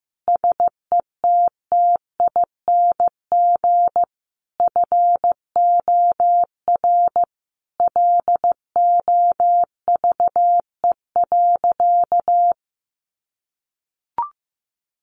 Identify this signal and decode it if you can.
Morse code, heard as a keyed tone, about 15 wpm, 700 Hz. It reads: SETTING FOR LOVE.
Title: Sentences from Top 1300 Words 15wpm